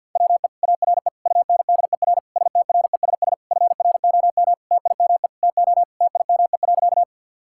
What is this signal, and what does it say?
Morse code, heard as a keyed tone, about 50 wpm, 700 Hz. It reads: WE ARE UNDER STRESS FROM TIME TO TIME.
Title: Sentences from Top 1000 Words - Review Part 6 - 50wpm